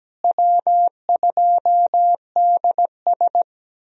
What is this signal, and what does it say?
Morse code, heard as a keyed tone, about 17 wpm, 700 Hz. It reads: W2DS
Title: USA Call Signs - 4 Characters - Repeated 3x - 17wpm